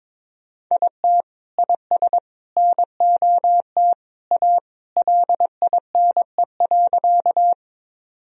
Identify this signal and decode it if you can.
Morse code, heard as a keyed tone, about 21 wpm, 700 Hz. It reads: IT IS NOT A LINE.
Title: Sentences from Top 1000 Words - Review Part 1 - 22wpm